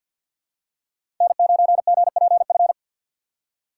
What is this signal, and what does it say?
Morse code, heard as a keyed tone, about 47 wpm, 700 Hz. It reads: N9GPF
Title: USA Call Signs - 4x Character Spacing - 50wpm